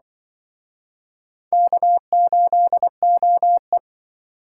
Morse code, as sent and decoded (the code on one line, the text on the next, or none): -.- ---.. --- .
K8OE